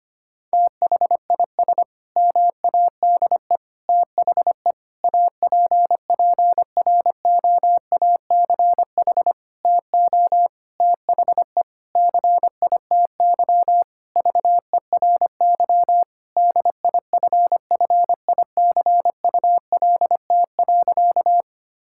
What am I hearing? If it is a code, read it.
Morse code, heard as a keyed tone, about 25 wpm, 700 Hz. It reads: THIS MADE THE APPROACH TO THE CITY VERY DIFFICULT.
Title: Sentences from Top 1000 Words - Review Part 6 - 25wpm